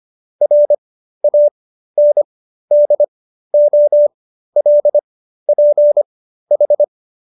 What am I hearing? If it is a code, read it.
Morse code, heard as a keyed tone, about 25 wpm, spaced as for 13 wpm, 590 Hz. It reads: RANDOLPH